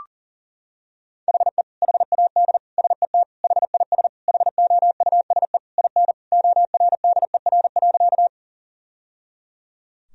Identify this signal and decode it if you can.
Morse code, heard as a keyed tone, about 37 wpm, 700 Hz. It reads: HE HAD SET HIS HOUSE IN ORDER.
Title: Sentences from Top 1000 Words - Review Part 2 - 40wpm